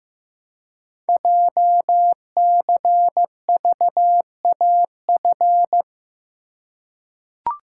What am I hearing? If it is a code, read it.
Morse code, heard as a keyed tone, about 15 wpm, 700 Hz. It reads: JCVAF
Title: Code Groups - 5 Letters - 15wpm